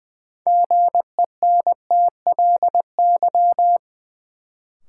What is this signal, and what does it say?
Morse code, heard as a keyed tone, about 20 wpm, 700 Hz. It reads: GENTLY
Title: Sentences from Top 1500 Words - New Word Drill Alt - 20wpm